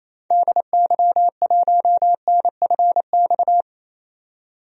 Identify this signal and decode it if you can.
Morse code, heard as a keyed tone, about 28 wpm, 700 Hz. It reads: DY1NFX
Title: DXCC-100 Call Signs - 6 Characters - 28wpm